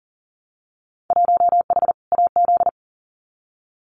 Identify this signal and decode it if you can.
Morse code, heard as a keyed tone, about 41 wpm, 700 Hz. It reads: 1H AZ
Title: Winter Field Day Exchanges 40wpm